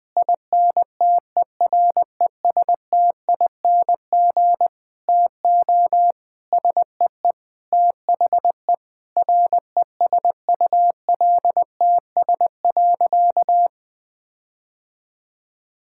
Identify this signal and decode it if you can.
Morse code, heard as a keyed tone, about 20 wpm, 700 Hz. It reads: INTERESTING TO SEE THE RESULTS.
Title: Sentences with 7-Word Phrases - 20wpm